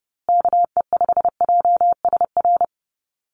Morse code, heard as a keyed tone, about 30 wpm, 700 Hz. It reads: KE5JSR
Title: USA Call Signs Repeated 3x 30wpm